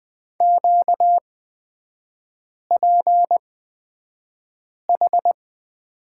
Morse code, heard as a keyed tone, about 20 wpm, spaced as for 5 wpm, 690 Hz. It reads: QPH
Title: 255 Lesson 41e - Period - Three Characters 20wpm